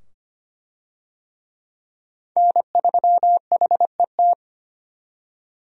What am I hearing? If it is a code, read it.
Morse code, heard as a keyed tone, about 25 wpm, 700 Hz. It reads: N3HET
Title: USA Call Signs - 5 Characters - 25wpm